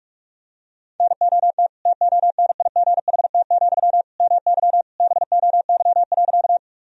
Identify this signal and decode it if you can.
Morse code, heard as a keyed tone, about 42 wpm, 700 Hz. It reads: NOT TONIGHT, MY BOY.
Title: Sentences from Top 1000 Words - Review Part 3 - 45wpm